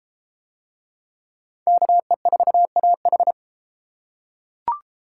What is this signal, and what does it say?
Morse code, heard as a keyed tone, about 31 wpm, 700 Hz. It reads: KE4AH